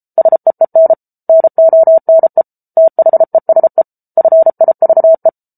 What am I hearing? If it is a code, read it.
Morse code, heard as a keyed tone, about 34 wpm, 650 Hz. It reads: SEEN NONE THESE FIVE